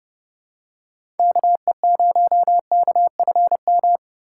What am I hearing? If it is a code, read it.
Morse code, heard as a keyed tone, about 30 wpm, 700 Hz. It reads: KE0KFM
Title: USA Call Signs - 6 Characters - 4x Character Spacing - 30wpm